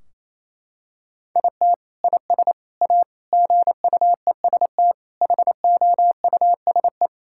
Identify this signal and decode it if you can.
Morse code, heard as a keyed tone, about 27 wpm, 700 Hz. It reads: IT IS A GUEST HOUSE
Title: Sentences from Top 1600 Words 28wpm